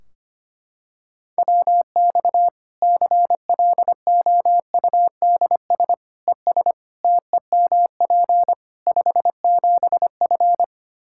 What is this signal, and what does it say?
Morse code, heard as a keyed tone, about 25 wpm, 700 Hz. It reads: WX CLOUDS ES TEMP 57F
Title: Practice QSO - Exchange 2 - Info - 25wpm